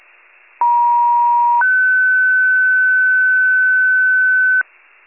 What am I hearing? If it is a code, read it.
Two-tone page, heard as a one-second tone, then a three-second tone, about 950 then 1500 Hz, with noise. It is heard from 0.6 to 4.6 s.
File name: wcalert